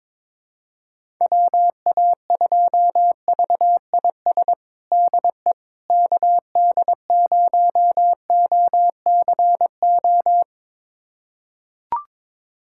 Morse code, heard as a keyed tone, about 22 wpm, 700 Hz. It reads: WA2VIS DE KD0OCO
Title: USA Callsign DE Callsign - 6 Characters - 22wpm